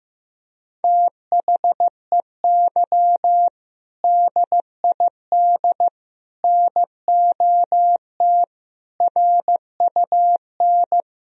Morse code, heard as a keyed tone, about 16 wpm, 700 Hz. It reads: THEY DID NOT RUN
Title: Sentences from Top 1000 Words - Review Part 1 - 15wpm